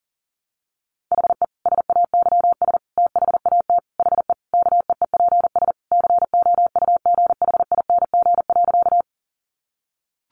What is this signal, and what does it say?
Morse code, heard as a keyed tone, about 40 wpm, 700 Hz. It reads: HE SAYS THAT HE KEEPS COUGHING.
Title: Additional Easy Phrases 40wpm